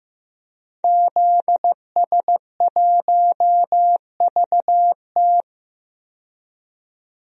Morse code, heard as a keyed tone, about 15 wpm, 700 Hz. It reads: ZS1VT
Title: DXCC-100 Call Signs - 5 Characters - 15wpm